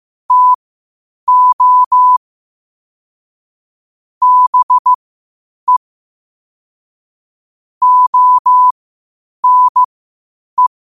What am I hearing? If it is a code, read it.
Morse code, heard as a keyed tone, about 15 wpm, spaced as for 8 wpm, 1000 Hz. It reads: TO BE ONE
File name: qotdfiles / 8wpm